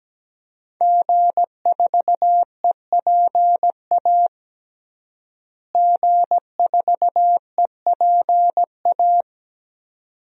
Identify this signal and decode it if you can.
Morse code, heard as a keyed tone, about 17 wpm, 700 Hz. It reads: G4EPA G4EPA
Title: DXCC-100 Call Signs - 5 Characters - Repeated 2x - 17wpm